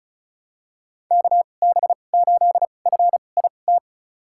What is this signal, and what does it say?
Morse code, heard as a keyed tone, about 36 wpm, spaced as for 25 wpm, 700 Hz. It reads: KB8FIT
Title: USA Call Signs - 2x Character Spacing - 35wpm